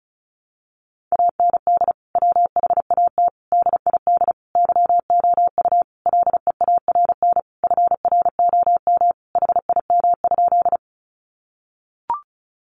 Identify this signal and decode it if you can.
Morse code, heard as a keyed tone, about 34 wpm, 700 Hz. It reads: AND WHAT DID YOU LEARN FROM HIM?
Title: Sentences from Top 300 Words 35wpm